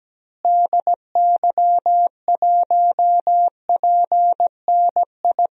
Morse code, heard as a keyed tone, about 17 wpm, 700 Hz. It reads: DY1PNI
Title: DXCC-100 Call Signs - 6 Characters - Repeated 2x - 17wpm